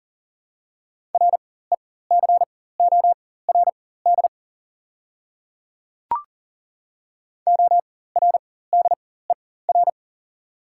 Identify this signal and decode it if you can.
Morse code, heard as a keyed tone, about 40 wpm, spaced as for 19 wpm, 690 Hz. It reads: RECORD ORDER
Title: 4x Character Spacing - Top 500 Words - 40wpm